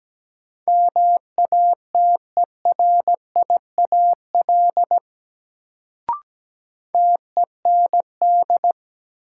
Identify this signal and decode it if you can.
Morse code, heard as a keyed tone, about 17 wpm, 700 Hz. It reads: MATERIAL TEND
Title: Sentences from Top 1000 Words - Review Part 5 - New Word Drill - 17wpm